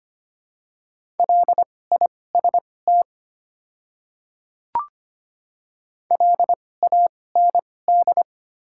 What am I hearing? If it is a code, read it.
Morse code, heard as a keyed tone, about 25 wpm, spaced as for 18 wpm, 700 Hz. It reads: LIST LAND
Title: 2x Character Spacing - Top 500 Words - 25wpm